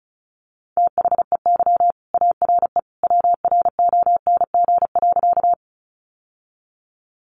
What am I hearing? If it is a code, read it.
Morse code, heard as a keyed tone, about 34 wpm, 700 Hz. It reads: THEY ARE WRONG.
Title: Sentences from Top 900 Words 35wpm